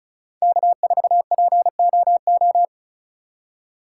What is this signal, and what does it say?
Morse code, heard as a keyed tone, about 34 wpm, 700 Hz. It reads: K4POO
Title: USA Call Signs - 5 Characters - 2x Character Spacing - 35wpm